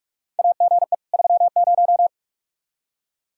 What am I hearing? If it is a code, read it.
Morse code, heard as a keyed tone, about 45 wpm, 700 Hz. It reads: AGE 30
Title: Practice QSO - Exchange 3 - More Info - 45wpm